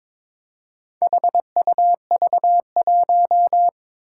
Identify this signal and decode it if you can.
Morse code, heard as a keyed tone, about 21 wpm, 700 Hz. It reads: HUV1
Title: Four Challenging Letter-Number Combinations 22wpm